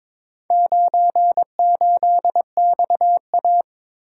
Morse code, heard as a keyed tone, about 23 wpm, 700 Hz. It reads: 98XA
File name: practice / 22wpm / Four Letter-Number Combinations - 22wpm